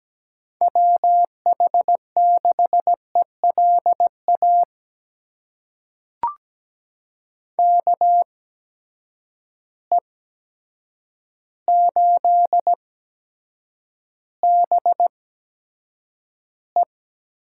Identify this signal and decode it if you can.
Morse code, heard as a keyed tone, about 17 wpm, spaced as for 8 wpm, 700 Hz. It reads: WH6ELA KE8BE
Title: USA Call Signs - 6 Characters - 8x Character Spacing - 17wpm